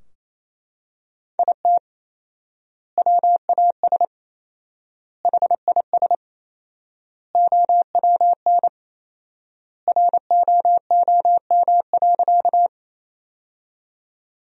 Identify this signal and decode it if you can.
Morse code, heard as a keyed tone, about 27 wpm, 700 Hz. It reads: IT WAS HIS OWN ROOM.
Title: 4x Word-Spacing - Sentences from Top 300 Words - 28wpm